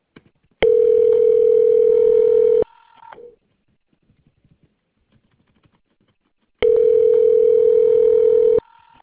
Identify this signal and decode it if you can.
Ringback tone (North America), 440 + 480 Hz.